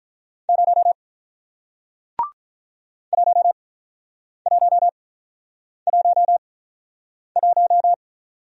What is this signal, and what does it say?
Morse code, heard as a keyed tone, about 47 wpm, spaced as for 9 wpm, 700 Hz. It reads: 0 1111